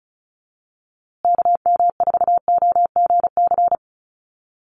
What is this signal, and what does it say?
Morse code, heard as a keyed tone, about 34 wpm, 700 Hz. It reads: KM4OGC
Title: USA Call Signs Repeated 3x 35wpm